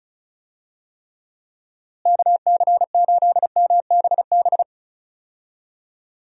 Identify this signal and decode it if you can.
Morse code, heard as a keyed tone, about 36 wpm, 700 Hz. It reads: KC8MBB
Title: USA Call Signs - 6 Characters - Repeated 2x - 35wpm